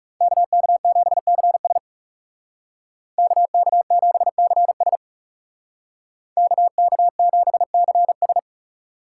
Morse code, heard as a keyed tone, about 37 wpm, 700 Hz. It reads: KK7CS KK7CS KK7CS
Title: Speed Racing - USA Call Signs 30wpm